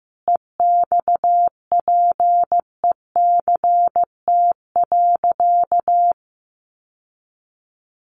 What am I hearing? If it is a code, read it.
Morse code, heard as a keyed tone, about 15 wpm, 700 Hz. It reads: EXPECT.
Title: Sentences from Top 1100 Words 15wpm